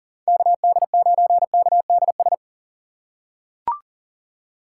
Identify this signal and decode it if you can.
Morse code, heard as a keyed tone, about 40 wpm, 690 Hz. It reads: KD9KDS
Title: USA Call Signs - 6 Characters - 2x Character Spacing - 40wpm